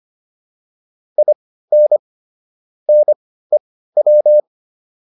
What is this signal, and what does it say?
Morse code, heard as a keyed tone, about 25 wpm, spaced as for 15 wpm, 600 Hz. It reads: IN NEW